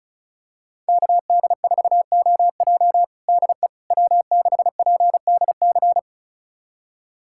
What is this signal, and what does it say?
Morse code, heard as a keyed tone, about 36 wpm, 700 Hz. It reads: KD4OJ DE W6PDC
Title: USA Callsign DE Callsign - 5 Characters - 35wpm